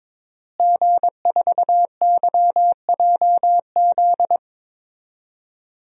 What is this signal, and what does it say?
Morse code, heard as a keyed tone, about 21 wpm, 700 Hz. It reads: G4YJZ
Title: DXCC-100 Call Signs - 5 Characters - Repeated 3x - 22wpm